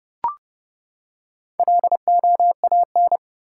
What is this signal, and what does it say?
Morse code, heard as a keyed tone, about 30 wpm, 700 Hz. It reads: LOAN
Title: Four Letter Words - 30wpm